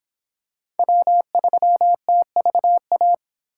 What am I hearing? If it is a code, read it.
Morse code, heard as a keyed tone, about 22 wpm, 700 Hz. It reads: W3TVA